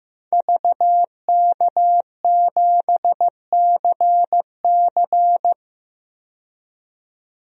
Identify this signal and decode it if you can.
Morse code, heard as a keyed tone, about 15 wpm, 700 Hz. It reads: VK7CC
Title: DXCC-100 Call Signs - 5 Characters - 15wpm